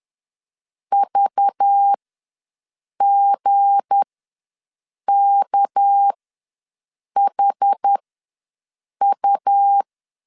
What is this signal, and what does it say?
Morse code, heard as a keyed tone, about 10 wpm, spaced as for 6 wpm, 800 Hz. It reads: VGKHU
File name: LEZIONI / 8 YJ45 / 8 YJ ws 10wpm 4043